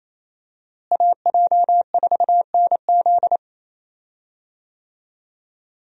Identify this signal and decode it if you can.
Morse code, heard as a keyed tone, about 28 wpm, 700 Hz. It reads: AJ4NZ
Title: USA Call Signs - 5 Characters - Repeated 2x - 28wpm